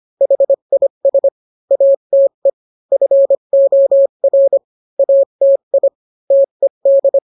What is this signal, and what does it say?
Morse code, heard as a keyed tone, about 25 wpm, 550 Hz. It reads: HIS ATE FOR ATI TED